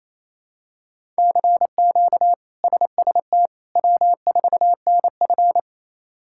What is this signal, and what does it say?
Morse code, heard as a keyed tone, about 28 wpm, 700 Hz. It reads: CQ SST W4NF
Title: K1USN SST QSO Practice - 28wpm